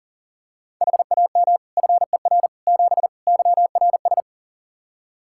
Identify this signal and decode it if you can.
Morse code, heard as a keyed tone, about 40 wpm, 700 Hz. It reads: HAM FER 7 YRS
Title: Practice QSO - Exchange 3 - More Info - 40wpm